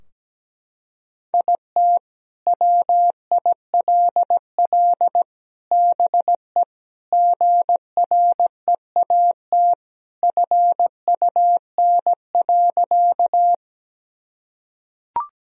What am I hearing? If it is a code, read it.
Morse code, heard as a keyed tone, about 17 wpm, 700 Hz. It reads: IT WILL BE GREAT FUN.
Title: Sentences from Top 1000 Words - Review Part 4 - 17wpm